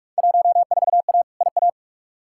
Morse code, heard as a keyed tone, about 43 wpm, 700 Hz. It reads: 14A IA
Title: Field Day Exchanges Encore - 45wpm